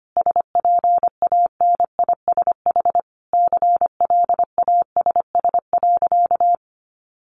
Morse code, heard as a keyed tone, about 25 wpm, 700 Hz. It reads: SPANISH CLASS.
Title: Additional Easy Phrases 25wpm